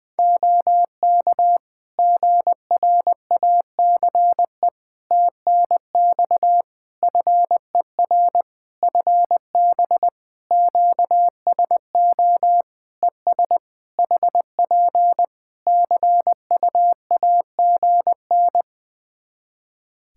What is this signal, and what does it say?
Morse code, heard as a keyed tone, about 20 wpm, 700 Hz. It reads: OK GRACE TNX FER FB QSO ES HP CUAGN
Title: Practice QSO - Exchange 4 - Ending - 20wpm